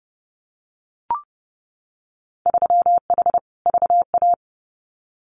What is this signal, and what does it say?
Morse code, heard as a keyed tone, about 30 wpm, 700 Hz. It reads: 3H VA
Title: Winter Field Day Exchanges 30wpm